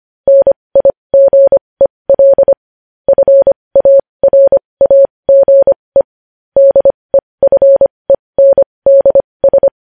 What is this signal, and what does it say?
Morse code, heard as a keyed tone, about 25 wpm, spaced as for 20 wpm, 550 Hz. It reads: NIGEL FARAGE DEFENDS
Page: news-headlines-in-morse-code-at-20-wpm